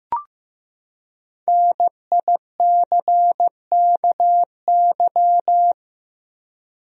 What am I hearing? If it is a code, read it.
Morse code, heard as a keyed tone, about 15 wpm, 700 Hz. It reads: NICKY
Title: USA Common Names - Max 5 Letters - 15wpm